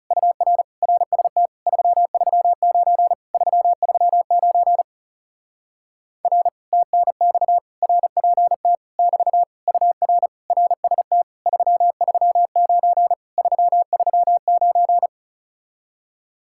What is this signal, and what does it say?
Morse code, heard as a keyed tone, about 40 wpm, 700 Hz. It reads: UR RST 339 339 R TNX RPT <BT> UR RST 339 339